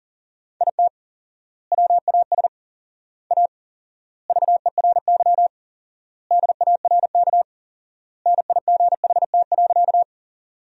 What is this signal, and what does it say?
Morse code, heard as a keyed tone, about 40 wpm, 700 Hz. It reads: IT WAS A VERY DARK NIGHT.
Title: 4x Word-Spacing - Sentences from Top 500 Words - 40wpm